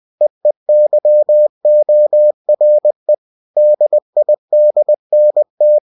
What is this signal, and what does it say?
Morse code, heard as a keyed tone, about 20 wpm, 600 Hz. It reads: EEYORE DIDNT
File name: cw-books / winnie-the-pooh / winnie-20wpm-page-0045